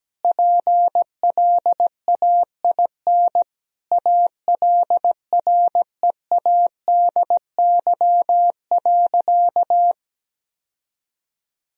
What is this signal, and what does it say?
Morse code, heard as a keyed tone, about 17 wpm, 700 Hz. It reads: PLAIN ALREADY.
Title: Sentences from Top 1000 Words - Review Part 2 - 17wpm